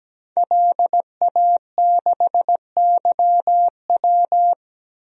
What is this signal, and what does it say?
Morse code, heard as a keyed tone, about 17 wpm, 700 Hz. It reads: LA6YW
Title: DXCC-100 Call Signs - 5 Characters - 17wpm